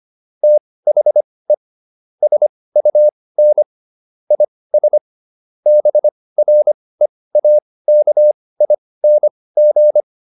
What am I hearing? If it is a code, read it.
Morse code, heard as a keyed tone, about 25 wpm, spaced as for 18 wpm, 600 Hz. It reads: THE SUN IS BREAKING
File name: QSO410 18